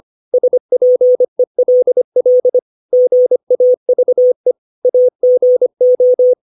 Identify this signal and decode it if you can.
Morse code, heard as a keyed tone, about 25 wpm, 500 Hz. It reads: SPELL GAVE AGO
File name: 632 words / 632 Words 25WPM